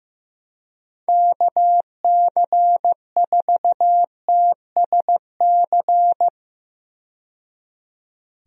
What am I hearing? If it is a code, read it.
Morse code, heard as a keyed tone, about 15 wpm, 700 Hz. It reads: KC4TSC
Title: USA Call Signs - 6 Characters - 15wpm